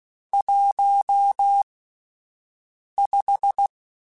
Morse code, heard as a keyed tone, about 16 wpm, spaced as for 5 wpm, 800 Hz. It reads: 15
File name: (spaced) / QSOs Spaced-mp3s / WD0BGP 1 @10 sp